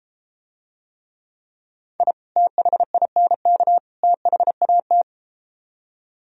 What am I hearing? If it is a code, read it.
Morse code, heard as a keyed tone, about 33 wpm, 700 Hz. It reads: I THINK THAT